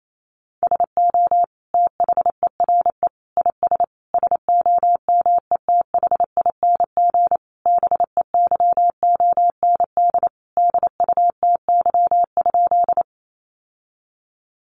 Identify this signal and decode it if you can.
Morse code, heard as a keyed tone, about 28 wpm, 700 Hz. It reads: SO THERE IS SOMETHING BEYOND DUTY?
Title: Sentences from Top 1500 Words 28wpm